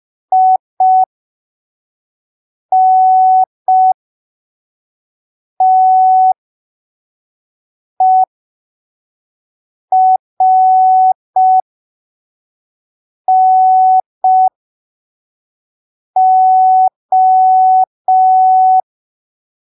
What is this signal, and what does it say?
Morse code, heard as a keyed tone, about 5 wpm, spaced as for 3 wpm, 750 Hz. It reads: INTERNO
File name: arquivos / texto / texto 11 05ppm